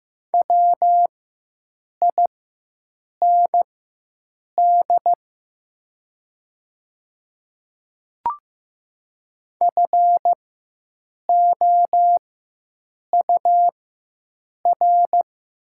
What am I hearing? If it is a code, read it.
Morse code, heard as a keyed tone, about 15 wpm, spaced as for 7 wpm, 690 Hz. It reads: WIND FOUR
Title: 4x Character Spacing - Top 500 Words - 15wpm